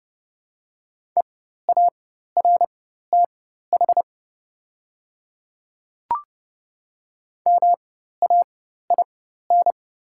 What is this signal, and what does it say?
Morse code, heard as a keyed tone, about 31 wpm, spaced as for 14 wpm, 700 Hz. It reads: EARTH MAIN
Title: 4x Character Spacing - Top 300 Words - 30wpm